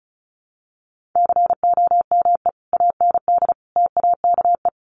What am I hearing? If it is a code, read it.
Morse code, heard as a keyed tone, about 34 wpm, 700 Hz. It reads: COME AND TAKE
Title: Sentences from Top 1500 Words 35wpm